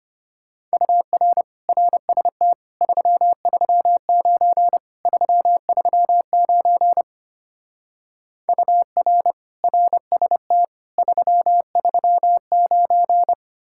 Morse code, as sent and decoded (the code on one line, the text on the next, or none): ..- .-. / .-. ... - / ...-- ...-- ----. / ...-- ...-- ----. / ..- .-. / .-. ... - / ...-- ...-- ----.
UR RST 339 339 UR RST 339